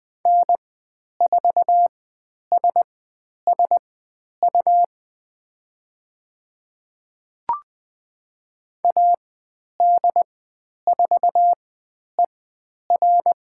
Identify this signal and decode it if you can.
Morse code, heard as a keyed tone, about 20 wpm, spaced as for 10 wpm, 700 Hz. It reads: N4SSU AD4ER
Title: 102 Lesson 17g - Letter u - Call Signs 20wpm